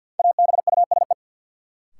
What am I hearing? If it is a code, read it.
Morse code, heard as a keyed tone, about 50 wpm, 700 Hz. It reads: ABUSE
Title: Sentences from Top 1300 Words - New Word Drill Alt - 50wpm